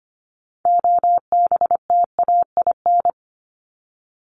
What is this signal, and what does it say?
Morse code, heard as a keyed tone, about 25 wpm, 700 Hz. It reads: OBTAIN
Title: ED Suffix - Top 100 Words with Base Words 25wpm